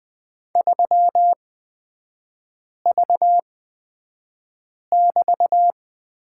Morse code, heard as a keyed tone, about 20 wpm, spaced as for 5 wpm, 700 Hz. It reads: 3V<BT>
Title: 224 Lesson 36e - BT - Three Characters 20wpm